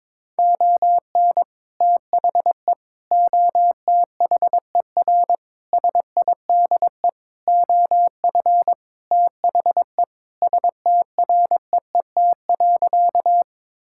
Morse code, heard as a keyed tone, about 22 wpm, 700 Hz. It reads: ON THE OTHER SIDE OF THE STREET.
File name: practice / 22wpm / Sentences with 2-Word Phrases - 22wpm